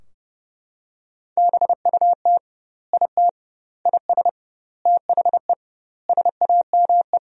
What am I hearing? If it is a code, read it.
Morse code, heard as a keyed tone, about 30 wpm, 700 Hz. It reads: BUT IT IS THE SAME
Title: 2x Word-Spacing - Sentences from Top 300 Words - 30wpm